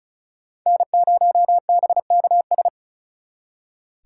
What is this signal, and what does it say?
Morse code, heard as a keyed tone, about 44 wpm, 700 Hz. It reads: N0BKS